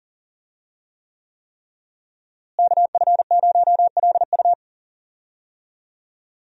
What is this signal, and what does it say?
Morse code, heard as a keyed tone, about 40 wpm, 700 Hz. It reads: KF0LU
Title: USA Call Signs - 5 Characters - Repeated 3x - 40wpm